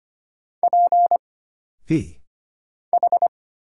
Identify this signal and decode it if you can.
Morse code, heard as a keyed tone, about 25 wpm, 700 Hz. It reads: P H
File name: Single Character (Letter:Number:Punctuation:Prosign) - Mind-Melt - 25wpm